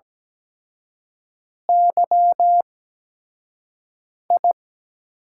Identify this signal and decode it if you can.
Morse code, heard as a keyed tone, about 17 wpm, spaced as for 5 wpm, 700 Hz. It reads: YI
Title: USA Call Signs - 8x Character Spacing - 17wpm